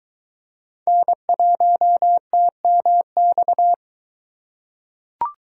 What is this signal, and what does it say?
Morse code, heard as a keyed tone, about 22 wpm, 700 Hz. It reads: N1TMX